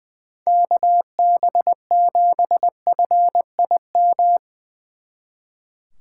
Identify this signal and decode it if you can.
Morse code, heard as a keyed tone, about 21 wpm, 700 Hz. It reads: KB7FIM
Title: Speed Racing - USA 2x3 Call Signs 20wpm